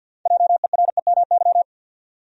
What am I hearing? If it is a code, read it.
Morse code, heard as a keyed tone, about 47 wpm, 700 Hz. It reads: JEREMY
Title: USA Common Names - 50wpm